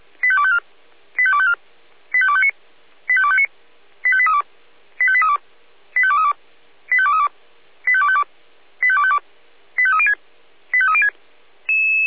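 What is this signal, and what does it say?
Five-tone selective call (ZVEI), heard as quick runs of five tones, tones of about 70 ms. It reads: 85254 85254 85259 85259 85832 85832 85232 85232 85252 85252 85306 85306
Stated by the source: ZVEI five-tone selective call received by SDR.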